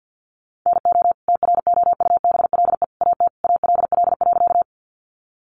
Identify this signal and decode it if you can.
Morse code, heard as a keyed tone, about 50 wpm, 700 Hz. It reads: NO TROUBLE AT ALL.
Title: Sentences from Top 1200 Words 50wpm